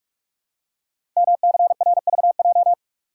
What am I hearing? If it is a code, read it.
Morse code, heard as a keyed tone, about 44 wpm, 700 Hz. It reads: MCRVJ
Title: Code Groups - 5 Letters - 45wpm